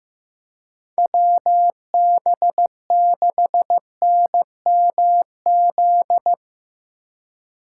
Morse code, heard as a keyed tone, about 15 wpm, 700 Hz. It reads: WB6NMZ
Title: USA Call Signs - 6 Characters - 2x Character Spacing - 15wpm